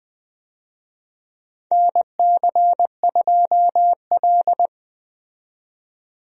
Morse code, heard as a keyed tone, about 20 wpm, 700 Hz. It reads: NC2L
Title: USA Call Signs - 4 Characters - 20wpm